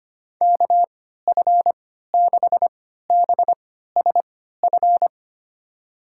Morse code, heard as a keyed tone, about 25 wpm, spaced as for 14 wpm, 700 Hz. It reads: KF6BSF
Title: USA Call Signs - 3x Character Spacing - 25wpm